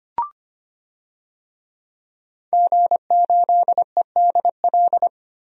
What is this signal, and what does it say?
Morse code, heard as a keyed tone, about 25 wpm, 700 Hz. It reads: G8EDL
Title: DXCC-100 Call Signs - 5 Characters - Repeated 2x - 25wpm